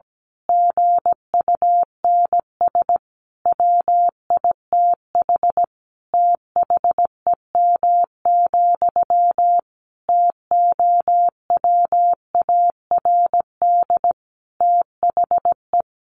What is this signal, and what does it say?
Morse code, heard as a keyed tone, about 17 wpm, 700 Hz. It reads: GUNS WITH THEM, TOWARD THE
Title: A Sound Of Thunder 17wpm